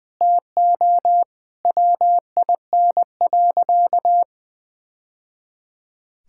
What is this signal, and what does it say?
Morse code, heard as a keyed tone, about 20 wpm, 700 Hz. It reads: TO WIN.
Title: Sentences with 3-Word Phrases - 20wpm